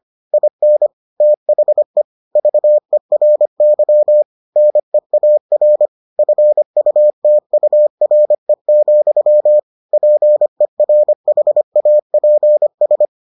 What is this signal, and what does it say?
Morse code, heard as a keyed tone, about 25 wpm, 600 Hz. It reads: IN THE VERY NEAR FUTURE, PERHAPS